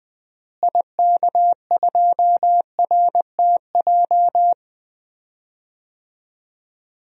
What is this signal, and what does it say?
Morse code, heard as a keyed tone, about 21 wpm, 700 Hz. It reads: IK2RTJ
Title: DXCC-100 Call Signs - 6 Characters - Repeated 2x - 20wpm